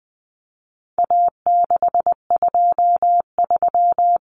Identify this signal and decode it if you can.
Morse code, heard as a keyed tone, about 20 wpm, 700 Hz. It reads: A623
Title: Common Cut Numbers - 4-Digit 20wpm